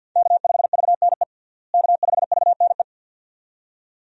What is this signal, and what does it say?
Morse code, heard as a keyed tone, about 50 wpm, 700 Hz. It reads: K5VNE K5VNE
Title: USA Call Signs - 5 Characters - Repeated 2x - 50wpm